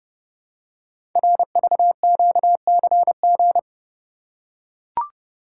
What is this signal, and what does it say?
Morse code, heard as a keyed tone, about 30 wpm, 700 Hz. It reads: RVQCG
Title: Code Groups - 5 Letters - 30wpm